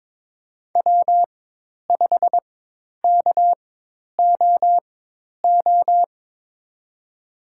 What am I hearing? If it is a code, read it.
Morse code, heard as a keyed tone, about 22 wpm, spaced as for 10 wpm, 700 Hz. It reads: W5KOO
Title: USA Call Signs - 5 Characters - 4x Character Spacing - 22wpm